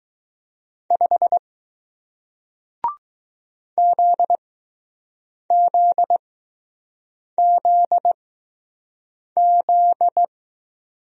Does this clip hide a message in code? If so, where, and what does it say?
Yes: Morse code from 0.9 to 10.2 s, heard as a keyed tone, about 20 wpm, spaced as for 6 wpm, 700 Hz. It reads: 5 ZZZZ